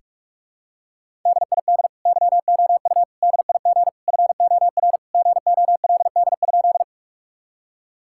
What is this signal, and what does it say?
Morse code, heard as a keyed tone, about 46 wpm, 700 Hz. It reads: DID YOU DIG FOR GOLD?
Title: Sentences from Top 1600 Words 45wpm